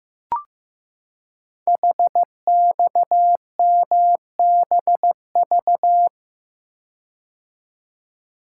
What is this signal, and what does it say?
Morse code, heard as a keyed tone, about 15 wpm, 700 Hz. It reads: HXMBV
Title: Code Groups - 5 Letters - 15wpm